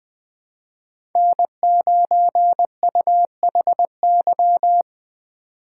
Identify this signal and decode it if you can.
Morse code, heard as a keyed tone, about 20 wpm, 700 Hz. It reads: N9UHY
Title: USA Call Signs - 5 Characters - 3x Character Spacing - 20wpm